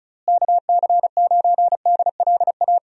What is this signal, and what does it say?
Morse code, heard as a keyed tone, about 36 wpm, 700 Hz. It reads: KC9DLA
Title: USA Call Signs - 6 Characters - Repeated 3x - 35wpm